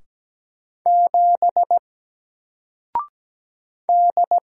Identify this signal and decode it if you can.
Morse code, heard as a keyed tone, about 17 wpm, 700 Hz. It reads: 7 D
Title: Single Letter-Number Alt 17wpm